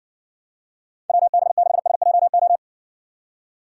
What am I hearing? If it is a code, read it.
Morse code, heard as a keyed tone, about 60 wpm, 700 Hz. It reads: WB6SJO